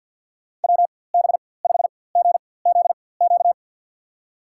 Morse code, heard as a keyed tone, about 52 wpm, spaced as for 23 wpm, 700 Hz. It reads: WB5GZQ
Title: USA Call Signs - 6 Characters - 4x Character Spacing - 50wpm